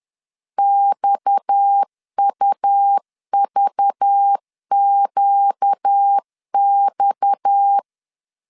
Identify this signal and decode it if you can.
Morse code, heard as a keyed tone, about 10 wpm, 800 Hz. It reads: XUVQX
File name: LEZIONI / 6 PKX0 / 6 PKX 10wpm 2426